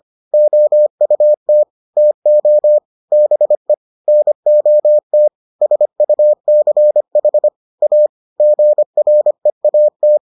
Morse code, heard as a keyed tone, about 25 wpm, 600 Hz. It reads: OUT TO BE NOT SUCH A GREAT